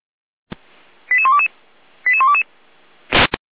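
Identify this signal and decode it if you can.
Five-tone selective call (ZVEI), heard as quick runs of five tones, tones of about 70 ms. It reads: 88122 88122
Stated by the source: ZVEI five-tone selective call received by SDR, from audio server of Feuerwehr Orlamünde.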